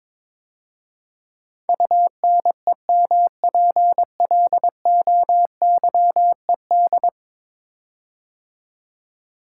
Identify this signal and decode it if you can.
Morse code, heard as a keyed tone, about 23 wpm, 700 Hz. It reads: UNEMPLOYED
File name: practice / 22wpm / 10 Letter Words - First 3 Letters Given - 22wpm